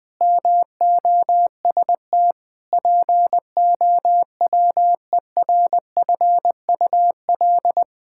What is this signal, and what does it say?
Morse code, heard as a keyed tone, about 20 wpm, 700 Hz. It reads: MOST POWERFUL
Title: Sentences with 7-Word Phrases - 20wpm